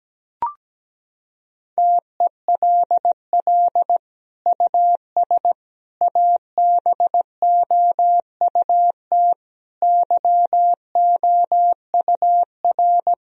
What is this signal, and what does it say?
Morse code, heard as a keyed tone, about 17 wpm, 700 Hz. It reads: TELL US ABOUT YOUR
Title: Sentences from Top 1000 Words - Review Part 4 - 17wpm